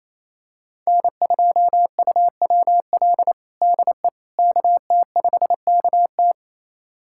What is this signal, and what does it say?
Morse code, heard as a keyed tone, about 28 wpm, 700 Hz. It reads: N2UWL DE KT5KT
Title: USA Callsign DE Callsign - 5 Characters - 28wpm